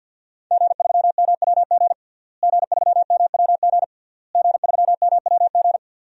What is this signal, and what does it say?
Morse code, heard as a keyed tone, about 49 wpm, 700 Hz. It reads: G3MWG G3MWG G3MWG
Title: DXCC-100 Call Signs - 5 Characters - Repeated 3x - 50wpm